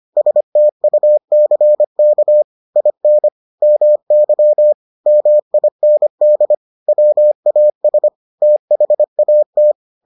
Morse code, heard as a keyed tone, about 25 wpm, 600 Hz. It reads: STUCK IN MY MIND WAS THAT